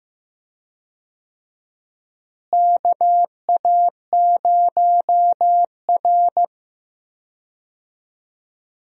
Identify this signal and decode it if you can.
Morse code, heard as a keyed tone, about 15 wpm, 700 Hz. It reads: KA0R
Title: USA Call Signs - 4 Characters - Repeated 3x - 15wpm